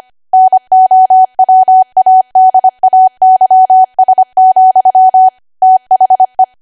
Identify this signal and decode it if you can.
Morse code, heard as a keyed tone, about 25 wpm, 750 Hz. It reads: NOWADAYS, THE
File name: Archive / 25 WPM / 25WPM